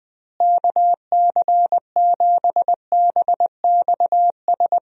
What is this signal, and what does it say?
Morse code, heard as a keyed tone, about 20 wpm, 700 Hz. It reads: KC7BXS
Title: USA Call Signs - 6 Characters - Repeated 3x - 20wpm